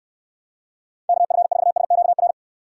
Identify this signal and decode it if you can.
Morse code, heard as a keyed tone, about 69 wpm, 700 Hz. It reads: KF4SQR